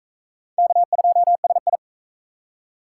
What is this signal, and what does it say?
Morse code, heard as a keyed tone, about 41 wpm, 700 Hz. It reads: K2SI